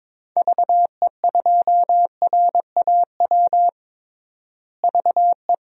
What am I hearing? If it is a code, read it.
Morse code, heard as a keyed tone, about 21 wpm, 700 Hz. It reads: VE2RAW VE
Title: DXCC-100 Call Signs - 6 Characters - Repeated 3x - 22wpm